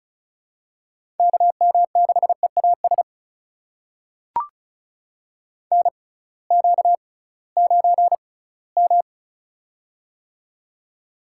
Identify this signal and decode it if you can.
Morse code, heard as a keyed tone, about 36 wpm, spaced as for 20 wpm, 700 Hz. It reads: KM6EAS NQ9M
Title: USA Call Signs - 6x Character Spacing - 35wpm